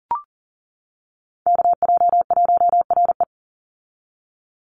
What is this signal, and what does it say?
Morse code, heard as a keyed tone, about 39 wpm, 700 Hz. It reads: KJ1RE
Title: Call Signs - 40wpm